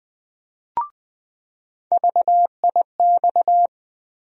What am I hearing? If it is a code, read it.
Morse code, heard as a keyed tone, about 20 wpm, 700 Hz. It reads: VIX
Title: USA Call Sign Postfixes - 3 Letters 20wpm